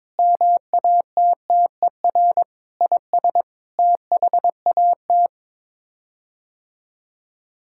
Morse code, heard as a keyed tone, about 22 wpm, 700 Hz. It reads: MATTER IS THAT
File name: practice / 22wpm / Sets of 8 Words - 22wpm